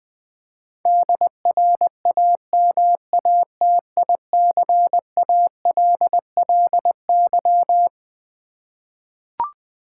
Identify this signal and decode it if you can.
Morse code, heard as a keyed tone, about 21 wpm, 700 Hz. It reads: DRAMATICALLY
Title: Twelve Letter Words - 20wpm